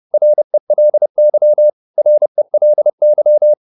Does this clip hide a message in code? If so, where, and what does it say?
Yes: Morse code from 0.1 to 3.5 s, heard as a keyed tone, about 30 wpm, 600 Hz. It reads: RELY RELY